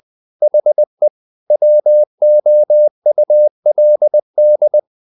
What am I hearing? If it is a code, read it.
Morse code, heard as a keyed tone, about 20 wpm, 600 Hz. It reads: HE WOULD
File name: the old man and the sea part 2 20wpm